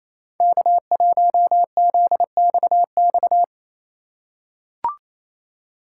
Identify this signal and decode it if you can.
Morse code, heard as a keyed tone, about 28 wpm, 700 Hz. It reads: K1ZXX
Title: USA Call Signs - 4x Character Spacing - 28wpm